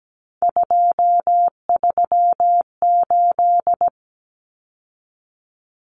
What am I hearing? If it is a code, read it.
Morse code, heard as a keyed tone, about 17 wpm, 700 Hz. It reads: 238
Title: Numbers - 3-Digit 17wpm